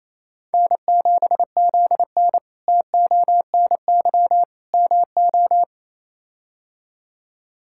Morse code, heard as a keyed tone, about 28 wpm, 700 Hz. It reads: N7ZN TONY MO
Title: K1USN SST QSO Practice - 28wpm